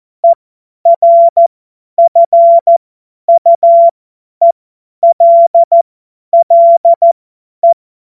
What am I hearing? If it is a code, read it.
Morse code, heard as a keyed tone, about 14 wpm, spaced as for 10 wpm, 680 Hz. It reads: ERFUELLE
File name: Witz-01-Juli-2025-50-F70